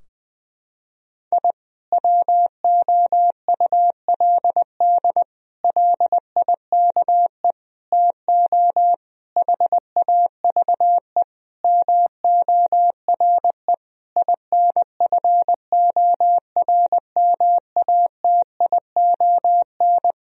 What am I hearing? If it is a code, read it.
Morse code, heard as a keyed tone, about 20 wpm, 700 Hz. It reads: I WOULD LIKE TO HAVE MORE INFORMATION
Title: Sentences from Top 1000 Words - Review Part 3 - 20wpm